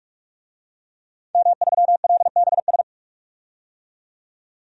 Morse code, heard as a keyed tone, about 44 wpm, 700 Hz. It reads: M3LBS
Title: DXCC-100 Call Signs - 5 Characters - 45wpm